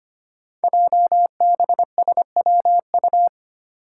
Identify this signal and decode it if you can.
Morse code, heard as a keyed tone, about 25 wpm, 700 Hz. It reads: JBSWU
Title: Code Groups - 5 Letter:Numbers - 25wpm